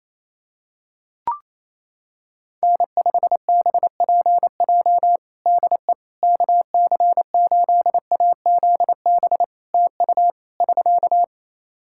Morse code, heard as a keyed tone, about 28 wpm, 700 Hz. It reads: N5BPJ DE KC8AZB TU <SK>
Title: K1USN SST QSO Practice 28wpm